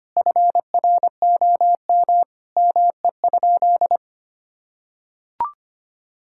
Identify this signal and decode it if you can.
Morse code, heard as a keyed tone, about 25 wpm, 700 Hz. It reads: FROM ME?
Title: Sentences from Top 1600 Words 25wpm